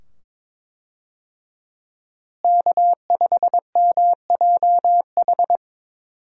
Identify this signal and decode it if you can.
Morse code, heard as a keyed tone, about 22 wpm, 700 Hz. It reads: K5MJH
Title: USA Call Signs - 5 Characters - Repeated 2x - 22wpm